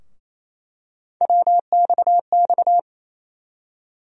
Morse code, heard as a keyed tone, about 27 wpm, 700 Hz. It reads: WXX
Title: USA Call Sign Postfixes - 3 Letters 28wpm